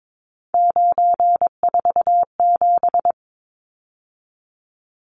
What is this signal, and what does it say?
Morse code, heard as a keyed tone, about 22 wpm, 700 Hz. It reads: 947
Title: Numbers - 3-Digit 22wpm